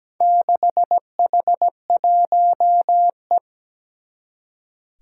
Morse code, heard as a keyed tone, about 17 wpm, 700 Hz. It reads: 6H1E
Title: Four Challenging Letter-Number Combinations 17wpm